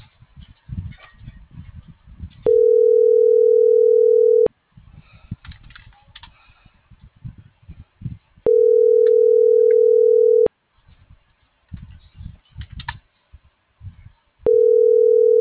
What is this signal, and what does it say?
Ringback tone (North America), 440 + 480 Hz.